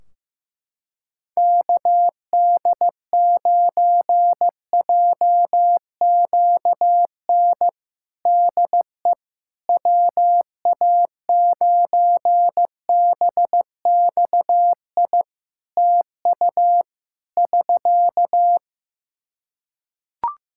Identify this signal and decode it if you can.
Morse code, heard as a keyed tone, about 15 wpm, 700 Hz. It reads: KD9JQN DE WA9BXI TU <SK>
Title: Practice QSO - Exchange 4 - Ending - 15wpm